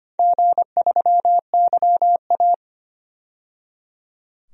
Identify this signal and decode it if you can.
Morse code, heard as a keyed tone, about 25 wpm, 700 Hz. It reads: G3YA
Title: DXCC-100 Call Signs - 4 Characters - 25wpm